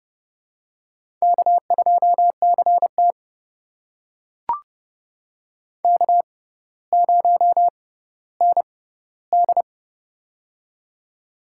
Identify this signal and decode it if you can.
Morse code, heard as a keyed tone, about 32 wpm, spaced as for 16 wpm, 700 Hz. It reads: K2CT K0ND
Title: USA Call Signs - 4 Characters - 6x Character Spacing - 30wpm